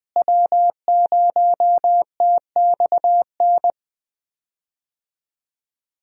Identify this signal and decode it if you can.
Morse code, heard as a keyed tone, about 20 wpm, 700 Hz. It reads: W0TXN
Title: USA Call Signs - 5 Characters - Repeated 3x - 20wpm